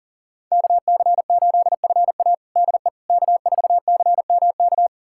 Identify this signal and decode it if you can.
Morse code, heard as a keyed tone, about 40 wpm, 700 Hz. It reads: KC8FA DE K4CMK
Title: USA Callsign DE Callsign - 5 Characters - 40wpm